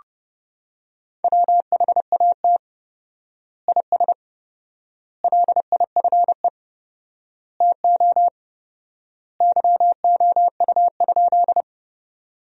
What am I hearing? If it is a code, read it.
Morse code, heard as a keyed tone, about 30 wpm, 700 Hz. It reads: WHAT IS LIFE TO YOU?
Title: 4x Word-Spacing - Sentences from Top 300 Words - 30wpm